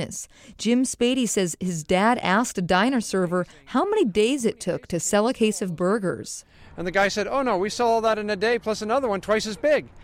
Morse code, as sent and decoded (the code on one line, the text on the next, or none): none